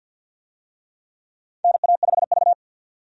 Morse code, heard as a keyed tone, about 52 wpm, 700 Hz. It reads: NA5V
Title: USA Call Signs - 4 Characters - 50wpm